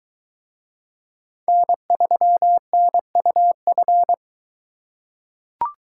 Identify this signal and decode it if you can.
Morse code, heard as a keyed tone, about 21 wpm, 700 Hz. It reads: N3NUF